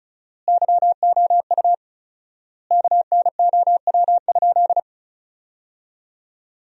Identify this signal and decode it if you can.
Morse code, heard as a keyed tone, about 34 wpm, 700 Hz. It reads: YOU KNOW?
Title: 4x Word-Spacing - Sentences from Top 200 Words - 35wpm